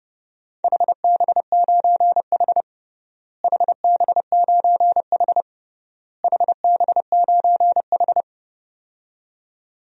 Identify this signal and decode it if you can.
Morse code, heard as a keyed tone, about 32 wpm, 700 Hz. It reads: HB9H HB9H HB9H
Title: DXCC-100 Call Signs - 4 Characters - Repeated 3x - 30wpm